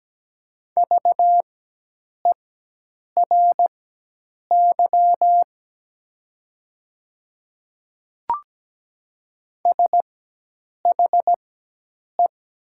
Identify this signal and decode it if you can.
Morse code, heard as a keyed tone, about 17 wpm, spaced as for 8 wpm, 700 Hz. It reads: VERY SHE
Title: 4x Character Spacing - Top 200 Words - 17wpm